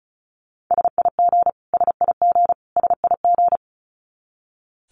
Morse code, heard as a keyed tone, about 34 wpm, 700 Hz. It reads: SIG SIG SIG
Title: Abbreviations Repeated 3x - 35wpm